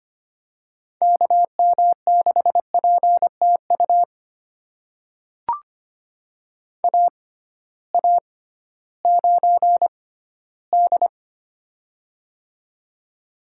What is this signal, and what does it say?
Morse code, heard as a keyed tone, about 25 wpm, spaced as for 15 wpm, 700 Hz. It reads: KM6PTU AA9D
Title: USA Call Signs - 6x Character Spacing - 25wpm